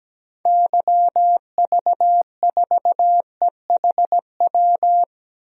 Morse code, heard as a keyed tone, about 17 wpm, 700 Hz. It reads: YV4EHW
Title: DXCC-100 Call Signs - 6 Characters - 17wpm